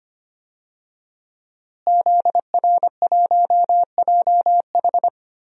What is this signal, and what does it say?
Morse code, heard as a keyed tone, about 25 wpm, 700 Hz. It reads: ZR1JH
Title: DXCC-100 Call Signs - 5 Characters - Repeated 2x - 25wpm